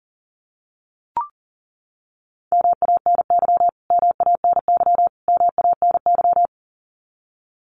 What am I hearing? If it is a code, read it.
Morse code, heard as a keyed tone, about 40 wpm, 700 Hz. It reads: MANY MANY MANY
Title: Top 500 Words Repeated 40wpm